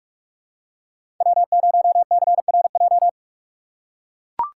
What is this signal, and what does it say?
Morse code, heard as a keyed tone, about 44 wpm, 700 Hz. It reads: W0CRJ
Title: USA Call Signs - 5 Characters - 6x Character Spacing - 45wpm